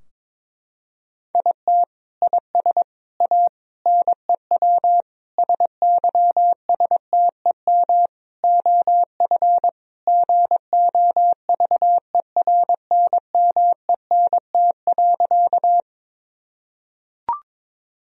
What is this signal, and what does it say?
Morse code, heard as a keyed tone, about 22 wpm, 700 Hz. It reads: IT IS A NEW SYSTEM OF GOVERNMENT.
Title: Sentences from Top 1000 Words - Review Part 3 - 22wpm